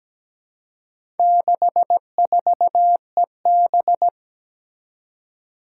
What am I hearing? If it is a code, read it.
Morse code, heard as a keyed tone, about 17 wpm, 700 Hz. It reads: 64EB
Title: Four Challenging Letter-Number Combinations 17wpm